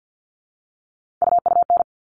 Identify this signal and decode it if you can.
Morse code, heard as a keyed tone, about 50 wpm, 700 Hz. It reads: UUN
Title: All Cut Numbers - 3-Digit 50wpm